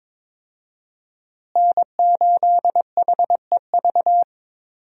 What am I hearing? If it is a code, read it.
Morse code, heard as a keyed tone, about 22 wpm, 700 Hz. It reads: N8HEV
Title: USA Call Signs - 5 Characters - Repeated 3x - 22wpm